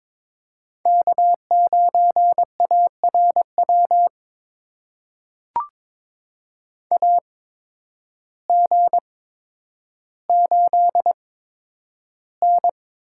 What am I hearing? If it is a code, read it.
Morse code, heard as a keyed tone, about 21 wpm, spaced as for 10 wpm, 700 Hz. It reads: K9ARW AG8N
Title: USA Call Signs - 8x Character Spacing - 22wpm